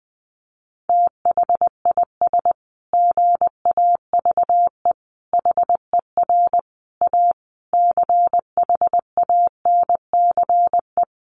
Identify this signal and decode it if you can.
Morse code, heard as a keyed tone, about 20 wpm, 700 Hz. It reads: THIS GAVE HER A CHANCE
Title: Sentences from Top 1400 Words 20wpm